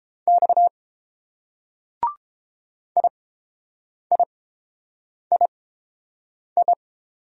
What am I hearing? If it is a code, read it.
Morse code, heard as a keyed tone, about 30 wpm, spaced as for 7 wpm, 700 Hz. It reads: X IIII